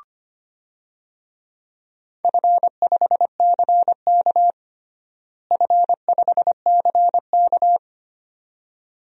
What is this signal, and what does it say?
Morse code, heard as a keyed tone, about 25 wpm, 700 Hz. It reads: F5CK F5CK
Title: DXCC-100 Call Signs - 4 Characters - Repeated 2x - 25wpm